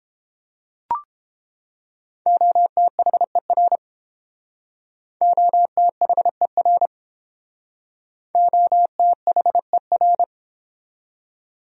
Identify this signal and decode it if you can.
Morse code, heard as a keyed tone, about 29 wpm, 700 Hz. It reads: OTHER OTHER OTHER